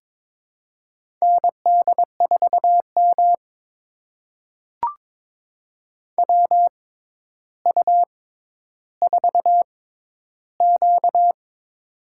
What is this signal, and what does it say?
Morse code, heard as a keyed tone, about 22 wpm, spaced as for 11 wpm, 700 Hz. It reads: ND4M WU4Q
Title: USA Call Signs - 4 Characters - 6x Character Spacing - 22wpm